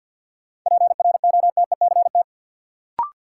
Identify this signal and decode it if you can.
Morse code, heard as a keyed tone, about 48 wpm, 700 Hz. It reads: PROTECT
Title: Sentences from Top 900 Words - New Word Drill - 50wpm